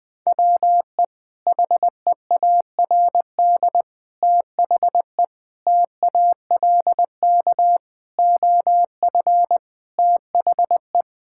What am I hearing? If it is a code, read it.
Morse code, heard as a keyed tone, about 20 wpm, 700 Hz. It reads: WE HEARD THE TALK OF THE
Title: Sentences with 5-Word Phrases - 20wpm